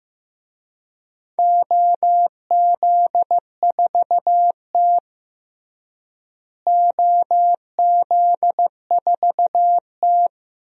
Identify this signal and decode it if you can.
Morse code, heard as a keyed tone, about 16 wpm, 700 Hz. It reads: OZ4T OZ4T
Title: DXCC-100 Call Signs - 4 Characters - Repeated 2x - 15wpm